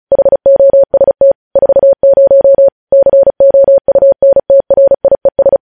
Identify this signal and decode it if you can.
Morse code, heard as a keyed tone, about 36 wpm, 550 Hz. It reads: HOST 40 COUNTRIES
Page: News headlines in Morse code at 35 wpm